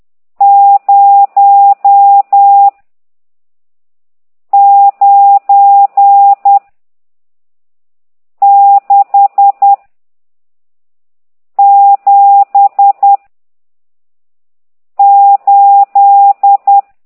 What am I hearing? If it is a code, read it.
Morse code, heard as a keyed tone, about 10 wpm, 800 Hz. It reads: 0 9 6 7 8
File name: ct2idw / exerc14